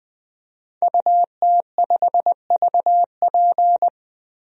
Morse code, heard as a keyed tone, about 20 wpm, 700 Hz. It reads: UT5VP
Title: DXCC-100 Call Signs - 5 Characters - Repeated 3x - 20wpm